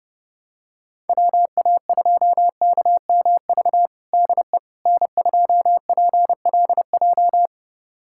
Morse code, heard as a keyed tone, about 30 wpm, 700 Hz. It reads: WA2KMV DE N2PLJ
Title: USA Callsign DE Callsign 30wpm